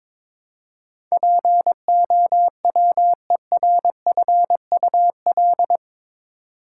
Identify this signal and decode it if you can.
Morse code, heard as a keyed tone, about 22 wpm, 700 Hz. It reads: POWERFUL
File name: practice / 22wpm / Eight Letter Words - 22wpm